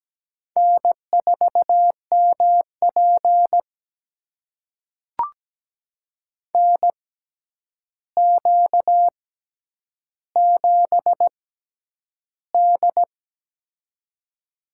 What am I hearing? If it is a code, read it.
Morse code, heard as a keyed tone, about 17 wpm, spaced as for 9 wpm, 700 Hz. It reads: N4MP NQ7D
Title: USA Call Signs - 4 Characters - 6x Character Spacing - 17wpm